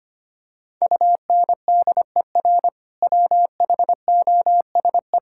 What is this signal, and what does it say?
Morse code, heard as a keyed tone, about 25 wpm, 700 Hz. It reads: UNDER WHOSE
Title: Sentences from Top 1600 Words 25wpm